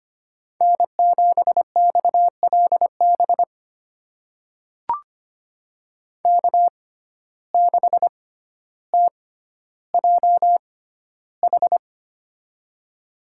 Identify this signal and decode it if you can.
Morse code, heard as a keyed tone, about 25 wpm, spaced as for 13 wpm, 700 Hz. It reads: N7XLB K6TJH
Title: USA Call Signs - 5 Characters - 6x Character Spacing - 25wpm